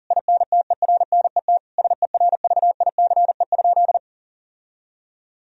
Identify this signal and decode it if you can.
Morse code, heard as a keyed tone, about 37 wpm, 700 Hz. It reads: INTERNET SERVICE?
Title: Sentences from Top 1000 Words - Review Part 4 - 40wpm